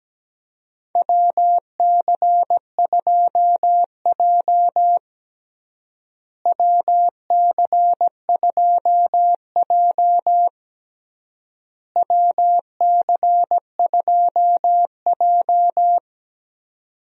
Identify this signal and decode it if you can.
Morse code, heard as a keyed tone, about 17 wpm, 700 Hz. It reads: WC2J WC2J WC2J
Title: USA Call Signs - 4 Characters - Repeated 3x - 17wpm